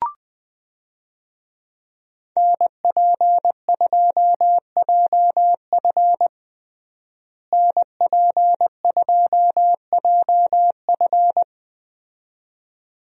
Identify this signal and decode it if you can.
Morse code, heard as a keyed tone, about 20 wpm, 700 Hz. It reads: NP2JF NP2JF
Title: USA Call Signs - 5 Characters - Repeated 2x - 20wpm